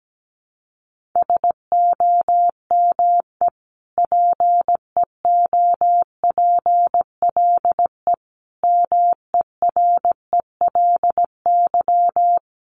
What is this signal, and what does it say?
Morse code, heard as a keyed tone, about 17 wpm, 700 Hz. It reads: SOME PEOPLE MERELY
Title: Sentences from Top 1500 Words 17wpm